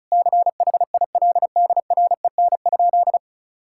Morse code, heard as a keyed tone, about 36 wpm, 700 Hz. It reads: CHILDREN?
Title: Sentences from Top 1000 Words - Review Part 6 - 35wpm